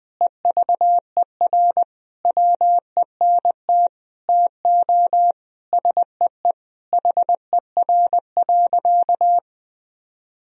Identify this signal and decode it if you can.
Morse code, heard as a keyed tone, about 21 wpm, 700 Hz. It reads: EVER WENT TO SEE HER.
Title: Sentences from Top 1000 Words - Review Part 1 - 20wpm